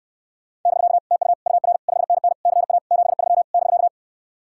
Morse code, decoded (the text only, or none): <BK> TU UR 5NN CA 73 <BK>